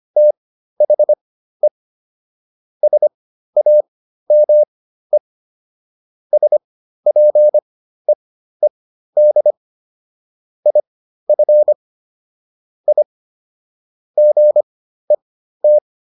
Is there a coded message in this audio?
Yes: Morse code from 0.2 to 15.8 s, heard as a keyed tone, about 25 wpm, spaced as for 13 wpm, 600 Hz. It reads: THE SAME SPEED IF I GET